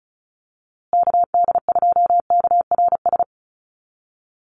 Morse code, heard as a keyed tone, about 34 wpm, 700 Hz. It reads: KD2KRS
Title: USA Call Signs Repeated 2x 35wpm